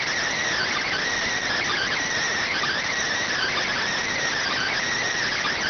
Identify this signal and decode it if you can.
Emergency Alert System (SAME) tones (header bursts), heard as harsh buzzing bursts. It is heard from 0.5 to 4.8 s.
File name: findings / CT CORONAL 710mm 10mm scout.gen-35 rx